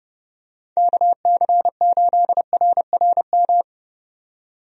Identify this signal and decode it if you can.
Morse code, heard as a keyed tone, about 31 wpm, 700 Hz. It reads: KC8RRM
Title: USA Call Signs - 6 Characters - 30wpm